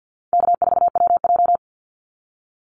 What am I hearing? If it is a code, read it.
Morse code, heard as a keyed tone, about 50 wpm, 700 Hz. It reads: K4WJ
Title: Easy Call Signs Repeated 3x - 50wpm